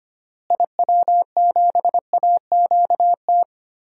Morse code, heard as a keyed tone, about 25 wpm, 700 Hz. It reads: IW7AQT
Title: DXCC-100 Call Signs - 6 Characters - Repeated 2x - 25wpm